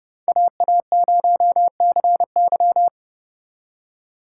Morse code, heard as a keyed tone, about 35 wpm, 700 Hz. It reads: AA0CY